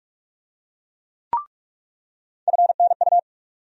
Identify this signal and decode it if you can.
Morse code, heard as a keyed tone, about 47 wpm, 700 Hz. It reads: FNU
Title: USA Call Sign Postfixes - 3 Letters 45wpm